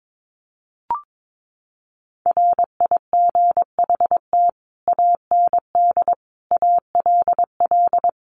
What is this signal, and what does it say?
Morse code, heard as a keyed tone, about 22 wpm, 700 Hz. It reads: RIGHT AND ALL
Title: A Sound Of Thunder 22wpm